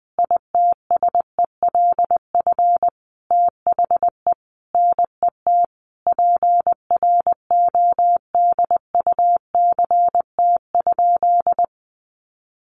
Morse code, heard as a keyed tone, about 20 wpm, 700 Hz. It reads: ITSELF THE NET PRODUCT?
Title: Sentences from Top 1400 Words 20wpm